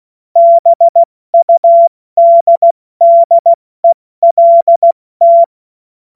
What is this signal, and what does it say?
Morse code, heard as a keyed tone, about 16 wpm, spaced as for 14 wpm, 680 Hz. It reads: BUDDELT
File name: Morsekurs / Witze / Witz 25. Januar 2022, 70 F80